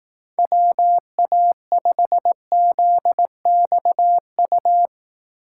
Morse code, heard as a keyed tone, about 18 wpm, 700 Hz. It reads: WA5ZXU
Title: Speed Racing - USA 2x3 Call Signs 15wpm